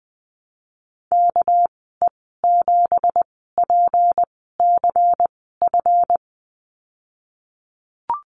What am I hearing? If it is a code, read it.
Morse code, heard as a keyed tone, about 20 wpm, spaced as for 15 wpm, 700 Hz. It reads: KE7PCF
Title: 219 Lesson 35g - Number 0 - Call Signs 20wpm